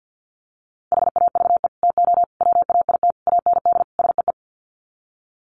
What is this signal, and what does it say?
Morse code, heard as a keyed tone, about 48 wpm, 700 Hz. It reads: HAVE TO WAIT AND SEE
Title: Sets of 5 Words Encore2 50wpm